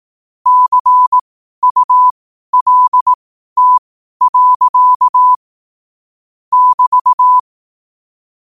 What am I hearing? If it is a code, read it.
Morse code, heard as a keyed tone, about 18 wpm, spaced as for 12 wpm, 1000 Hz. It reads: CULT. <BT>
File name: qotdfiles / 12wpm